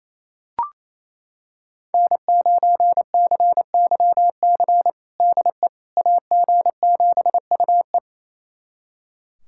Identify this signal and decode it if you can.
Morse code, heard as a keyed tone, about 27 wpm, 700 Hz. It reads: N9CYC DE AG7UE
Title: USA Callsign DE Callsign - 5 Characters - 28wpm